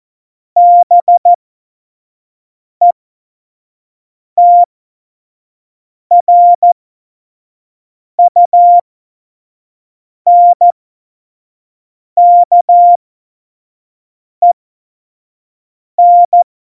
Morse code, heard as a keyed tone, about 14 wpm, spaced as for 5 wpm, 700 Hz. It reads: BETRUNKEN
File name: Morsekurs / Witze / Witz 22. Mai 2018, 25 F70